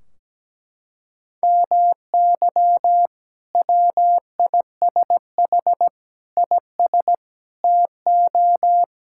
Morse code, heard as a keyed tone, about 17 wpm, 700 Hz. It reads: MY WISH IS TO
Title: Sentences from Top 1600 Words 17wpm